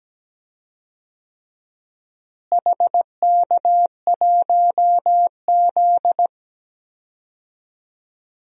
Morse code, heard as a keyed tone, about 17 wpm, 700 Hz. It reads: HK1Z
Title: DXCC-100 Call Signs - 4 Characters - Repeated 3x - 17wpm